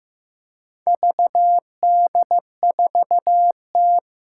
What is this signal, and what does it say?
Morse code, heard as a keyed tone, about 15 wpm, 700 Hz. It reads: VD4T
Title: Four Challenging Letter-Number Combinations 15wpm